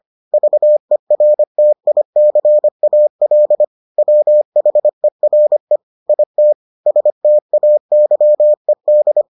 Morse code, heard as a keyed tone, about 25 wpm, 600 Hz. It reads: VERTICAL WHERE IT STAYED